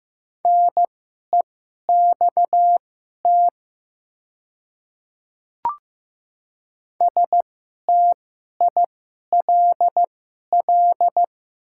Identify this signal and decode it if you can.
Morse code, heard as a keyed tone, about 15 wpm, spaced as for 11 wpm, 700 Hz. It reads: NEXT STILL
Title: 2x Character Spacing - Top 300 Words - 15wpm